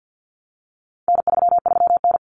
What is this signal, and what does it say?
Morse code, heard as a keyed tone, about 50 wpm, 700 Hz. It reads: N33N
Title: Common Cut Numbers - 4-Digit 50wpm